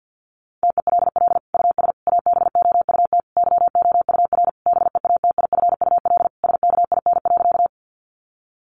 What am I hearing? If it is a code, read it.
Morse code, heard as a keyed tone, about 49 wpm, 700 Hz. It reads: TELL US ABOUT YOUR BEATIFUL SKIN.
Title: Sentences from Top 1300 Words 50wpm